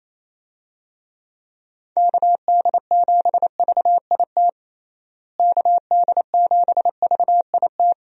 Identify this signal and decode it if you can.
Morse code, heard as a keyed tone, about 28 wpm, 700 Hz. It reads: KD7VIT KD7VIT
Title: USA Call Signs - 6 Characters - Repeated 2x - 28wpm